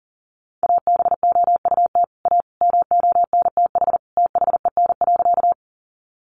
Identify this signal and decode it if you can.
Morse code, heard as a keyed tone, about 40 wpm, 700 Hz. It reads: ABOUT A MONTH THEN.
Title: Sentences from Top 800 Words 40wpm